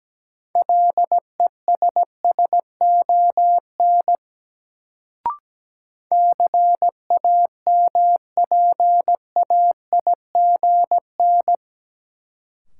Morse code, heard as a keyed tone, about 17 wpm, 700 Hz. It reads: LESSON CAMPAIGN
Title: Sentences from Top 1200 Words - New Word Drill - 17wpm